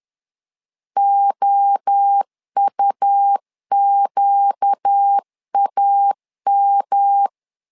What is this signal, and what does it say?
Morse code, heard as a keyed tone, about 10 wpm, 800 Hz. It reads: OUQAM
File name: LEZIONI / 4 Q 10wpm 2210